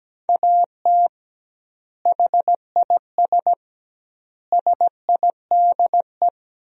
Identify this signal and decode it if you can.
Morse code, heard as a keyed tone, about 17 wpm, 700 Hz. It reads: AT HIS SIDE
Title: 2x Word-Spacing - Sentences from Top 500 Words - 17wpm